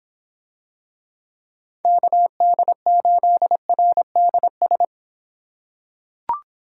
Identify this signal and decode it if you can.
Morse code, heard as a keyed tone, about 26 wpm, 700 Hz. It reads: KD8RDS